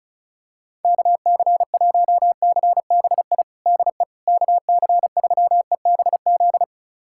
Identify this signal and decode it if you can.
Morse code, heard as a keyed tone, about 34 wpm, 700 Hz. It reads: KC1CBI DE KC3EBZ
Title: USA Callsign DE Callsign - 6 Characters - 35wpm